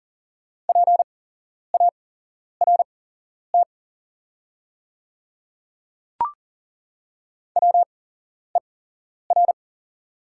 Morse code, heard as a keyed tone, about 42 wpm, spaced as for 11 wpm, 700 Hz. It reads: PART WER
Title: 8x Character Spacing - Top 200 Words - 40wpm